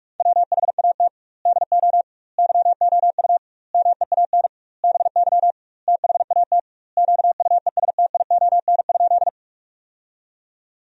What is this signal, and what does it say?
Morse code, heard as a keyed tone, about 46 wpm, 700 Hz. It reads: WHAT DO YOU MEAN BY THAT QUESTION?
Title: 2x Word-Spacing - Sentences from Top 400 Words - 45wpm